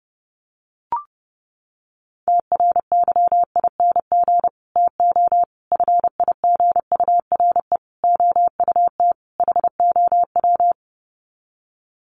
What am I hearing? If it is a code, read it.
Morse code, heard as a keyed tone, about 30 wpm, 700 Hz. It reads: TRYING TO FIGURE OUT HOW
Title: Sets of 5 Words Encore 30wpm